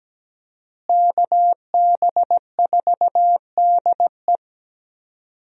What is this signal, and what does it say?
Morse code, heard as a keyed tone, about 17 wpm, 700 Hz. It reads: KB4DE
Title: USA Call Signs - 5 Characters - Repeated 3x - 17wpm